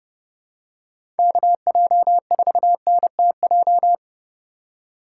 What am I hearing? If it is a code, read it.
Morse code, heard as a keyed tone, about 30 wpm, 700 Hz. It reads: KJ4NTJ
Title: USA Call Signs - 6 Characters - 6x Character Spacing - 30wpm